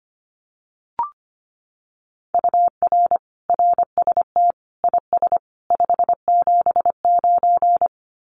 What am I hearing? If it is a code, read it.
Morse code, heard as a keyed tone, about 25 wpm, 700 Hz. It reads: UR RST IS 579
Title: Common QSO Phrases 25wpm